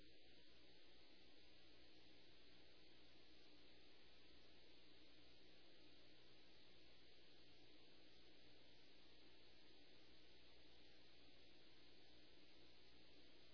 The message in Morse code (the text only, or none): none